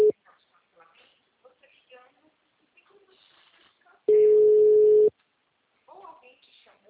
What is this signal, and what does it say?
Ringback tone (Europe, 425 Hz), with speech.
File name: out- -2014- - - .41592